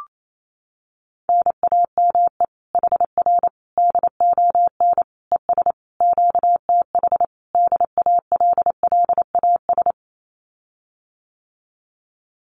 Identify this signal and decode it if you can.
Morse code, heard as a keyed tone, about 28 wpm, 700 Hz. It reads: NAME HR DON ES QTH DALLAS
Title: Common QSO Phrases Encore 28wpm